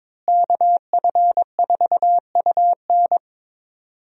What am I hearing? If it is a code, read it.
Morse code, heard as a keyed tone, about 22 wpm, 700 Hz. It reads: KF4UN
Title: USA Call Signs - 5 Characters - Repeated 3x - 22wpm